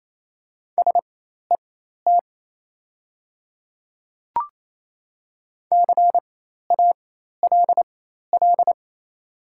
Morse code, heard as a keyed tone, about 28 wpm, spaced as for 13 wpm, 700 Hz. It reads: SET CALL
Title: 4x Character Spacing - Top 200 Words - 28wpm